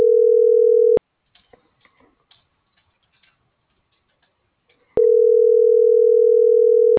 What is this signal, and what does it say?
Ringback tone (North America), 440 + 480 Hz.